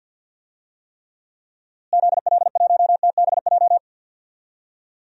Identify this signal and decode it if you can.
Morse code, heard as a keyed tone, about 53 wpm, 700 Hz. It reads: ZL1TBJ
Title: DXCC-100 Call Signs - 6 Characters - Repeated 2x - 50wpm